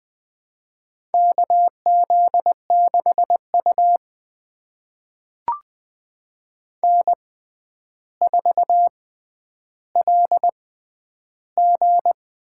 Morse code, heard as a keyed tone, about 21 wpm, spaced as for 10 wpm, 700 Hz. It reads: KZ6U N4LG
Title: USA Call Signs - 4 Characters - 6x Character Spacing - 20wpm